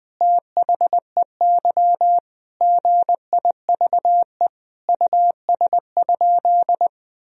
Morse code, heard as a keyed tone, about 20 wpm, 700 Hz. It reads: THEY GIVE US?
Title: Sentences from Top 1000 Words - Review Part 2 - 20wpm